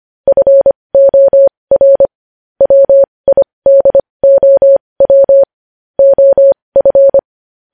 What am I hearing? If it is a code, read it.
Morse code, heard as a keyed tone, about 25 wpm, spaced as for 20 wpm, 550 Hz. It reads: FOR WIDOW OF
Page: News headlines in Morse code at 20 wpm